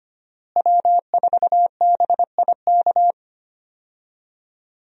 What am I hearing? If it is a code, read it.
Morse code, heard as a keyed tone, about 25 wpm, 700 Hz. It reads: W4BIK
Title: USA Call Signs - 5 Characters - Repeated 2x - 25wpm